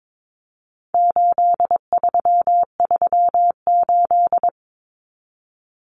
Morse code, heard as a keyed tone, about 22 wpm, 700 Hz. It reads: 8338
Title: Numbers - 4-Digit 22wpm